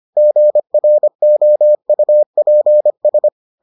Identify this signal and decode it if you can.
Morse code, heard as a keyed tone, about 25 wpm, 600 Hz. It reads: GROUPS